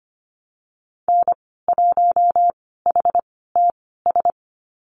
Morse code, heard as a keyed tone, about 25 wpm, spaced as for 16 wpm, 700 Hz. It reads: N1HTS
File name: Lesson 15g - Number 5 - Call Signs 25wpm